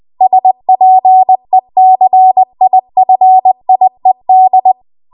Morse code, heard as a keyed tone, about 20 wpm, 750 Hz. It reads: SPECIFIED
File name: Archive / 20 WPM / 20WPM